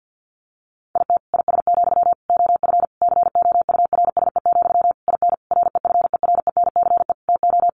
Morse code, heard as a keyed tone, about 50 wpm, 700 Hz. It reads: IT IS, OF COURSE, IN REFERENCE TO
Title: Sentences from Top 1200 Words 50wpm